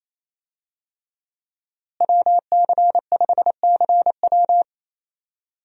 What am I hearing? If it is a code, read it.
Morse code, heard as a keyed tone, about 28 wpm, 700 Hz. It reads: WC5CW
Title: USA Call Signs - 5 Characters - Repeated 3x - 28wpm